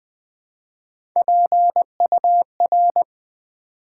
Morse code, heard as a keyed tone, about 20 wpm, 700 Hz. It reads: PUR
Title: Build A Word - 1 Letter At A Time - 4-Letter Words - 20wpm